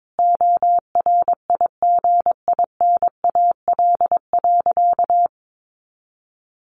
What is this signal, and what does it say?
Morse code, heard as a keyed tone, about 22 wpm, 700 Hz. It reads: ORIGINAL.
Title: Sentences from Top 900 Words 22wpm